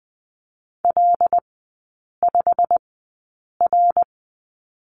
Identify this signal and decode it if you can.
Morse code, heard as a keyed tone, about 20 wpm, spaced as for 8 wpm, 690 Hz. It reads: L5R
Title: 087 Lesson 15e - Number 5 - Three Characters 20wpm